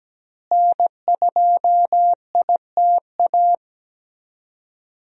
Morse code, heard as a keyed tone, about 17 wpm, 700 Hz. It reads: N2ITA
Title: USA Call Signs - 5 Characters - Repeated 3x - 17wpm